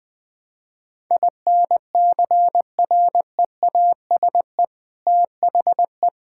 Morse code, heard as a keyed tone, about 21 wpm, 700 Hz. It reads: INCREASE THE
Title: Sentences from Top 1000 Words - Review Part 6 - 20wpm